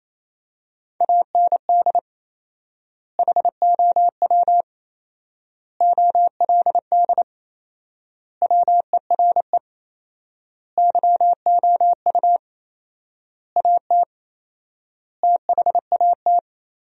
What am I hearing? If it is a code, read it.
Morse code, heard as a keyed tone, about 28 wpm, 700 Hz. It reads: AND HOW OLD WERE YOU AT THAT
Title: 4x Word-Spacing - Sentences from Top 200 Words - 28wpm